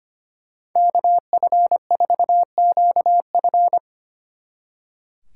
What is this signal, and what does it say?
Morse code, heard as a keyed tone, about 28 wpm, 700 Hz. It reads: KF4QF
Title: Speed Racing - USA 2x2 Call Signs 25wpm